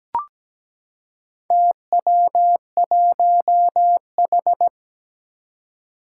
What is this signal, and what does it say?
Morse code, heard as a keyed tone, about 17 wpm, 700 Hz. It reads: TW1H
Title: Four Challenging Letter-Number Combinations 17wpm